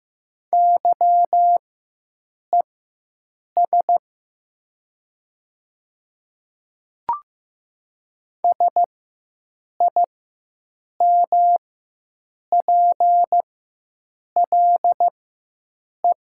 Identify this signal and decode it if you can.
Morse code, heard as a keyed tone, about 15 wpm, spaced as for 7 wpm, 700 Hz. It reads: YES SIMPLE
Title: 4x Character Spacing - Top 500 Words - 15wpm